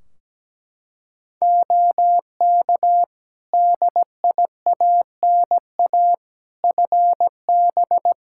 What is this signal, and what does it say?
Morse code, heard as a keyed tone, about 17 wpm, 700 Hz. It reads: OK DIANA FB
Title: Practice QSO - Exchange 2 - Info - 17wpm